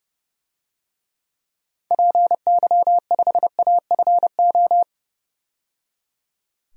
Morse code, heard as a keyed tone, about 30 wpm, 700 Hz. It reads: PY5AFO
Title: DXCC-100 Call Signs - 6 Characters - 30wpm